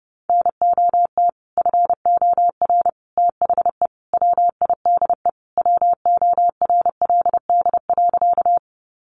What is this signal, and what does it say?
Morse code, heard as a keyed tone, about 30 wpm, 700 Hz. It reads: NOT FOR THE WIDE WORLD.
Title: Sentences from Top 900 Words 30wpm